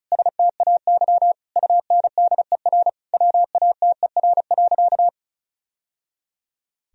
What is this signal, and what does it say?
Morse code, heard as a keyed tone, about 36 wpm, 700 Hz. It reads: STAY UNDER WATER.
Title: Sentences from Top 1000 Words - Review Part 2 - 35wpm